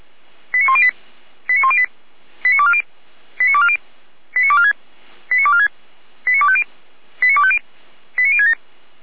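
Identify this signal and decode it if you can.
Five-tone selective call (ZVEI), heard as quick runs of five tones, tones of about 70 ms. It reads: 89108 89108 89240 89240 89246 89246 89250 89250 89067
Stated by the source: ZVEI five-tone call 89337 received by SDR, Feuerwehr Orlamünde.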